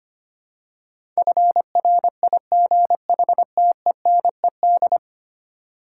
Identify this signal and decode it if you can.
Morse code, heard as a keyed tone, about 25 wpm, 700 Hz. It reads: FRIGHTENED
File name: Ten Letter Words - 25wpm